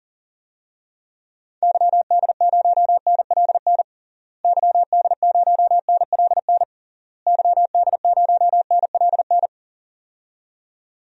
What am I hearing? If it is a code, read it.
Morse code, heard as a keyed tone, about 40 wpm, 700 Hz. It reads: YD0NLN YD0NLN YD0NLN
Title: DXCC-100 Call Signs - 6 Characters - Repeated 3x - 40wpm